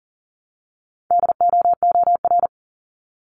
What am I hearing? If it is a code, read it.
Morse code, heard as a keyed tone, about 40 wpm, 700 Hz. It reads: DOOR
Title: Top 500 Words 40wpm